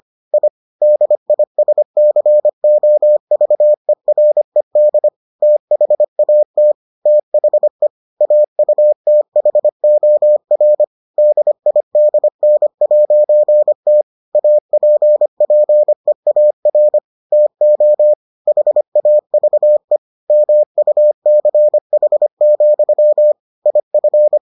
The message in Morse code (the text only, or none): I DISCOVERED THAT THE AUTHOR DIDN'T APPEAR TO HAVE MUCH, IF